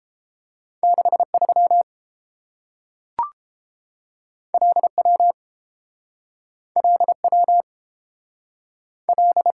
Morse code, heard as a keyed tone, about 30 wpm, 700 Hz. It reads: 63 LW LW L